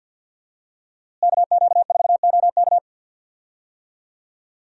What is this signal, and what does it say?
Morse code, heard as a keyed tone, about 53 wpm, 700 Hz. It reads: KQ4OK
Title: USA Call Signs - 5 Characters - Repeated 3x - 50wpm